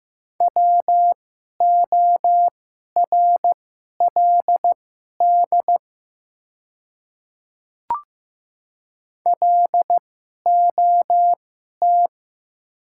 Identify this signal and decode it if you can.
Morse code, heard as a keyed tone, about 15 wpm, spaced as for 11 wpm, 700 Hz. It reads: WORLD LOT
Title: 2x Character Spacing - Top 500 Words - 15wpm